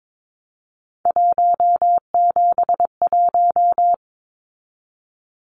Morse code, heard as a keyed tone, about 22 wpm, 700 Hz. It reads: 171
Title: Numbers - 3-Digit 22wpm